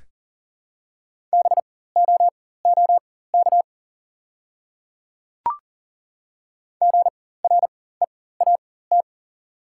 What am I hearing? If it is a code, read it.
Morse code, heard as a keyed tone, about 40 wpm, spaced as for 19 wpm, 690 Hz. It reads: BOOK GREAT
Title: 4x Character Spacing - Top 500 Words - 40wpm